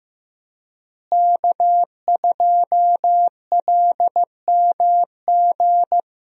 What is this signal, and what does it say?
Morse code, heard as a keyed tone, about 15 wpm, 700 Hz. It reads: K2LMG
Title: USA Call Signs - 5 Characters - Repeated 3x - 15wpm